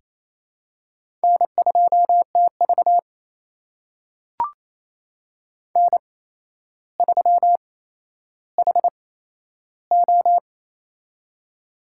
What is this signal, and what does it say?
Morse code, heard as a keyed tone, about 28 wpm, spaced as for 12 wpm, 700 Hz. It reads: N2TV N3HO
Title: USA Call Signs - 4 Characters - 8x Character Spacing - 28wpm